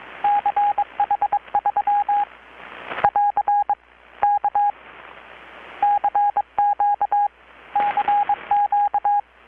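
Morse code, heard as a keyed tone, about 21 wpm, 820 Hz, with radio noise. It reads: CH3 <AR> K CQ CQ
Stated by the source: Morse from KPH's Night of Nights XX maritime radiotelegraphy event, received in Chiapas, Mexico about 3,800 km away on a Kenwood R-600 with a grounded 10 m long-wire antenna.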